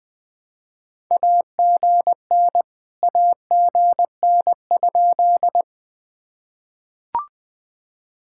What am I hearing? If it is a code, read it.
Morse code, heard as a keyed tone, about 22 wpm, 700 Hz. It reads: AGN AGN?
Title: POTA QSO Practice - 20wpm